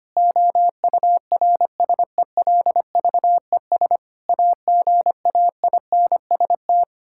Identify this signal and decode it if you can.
Morse code, heard as a keyed tone, about 25 wpm, 700 Hz. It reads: OURSELVES AGAINST
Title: Sentences from Top 1600 Words 25wpm